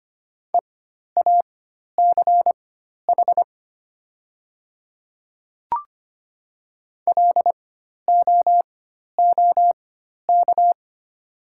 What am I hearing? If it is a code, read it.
Morse code, heard as a keyed tone, about 25 wpm, spaced as for 12 wpm, 700 Hz. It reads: EACH LOOK
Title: 4x Character Spacing - Top 100 Words - 25wpm